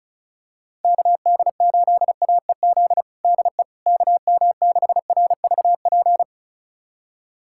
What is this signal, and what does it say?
Morse code, heard as a keyed tone, about 36 wpm, 700 Hz. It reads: KD8AEZ DE KM6RVP
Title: USA Callsign DE Callsign - 6 Characters - 35wpm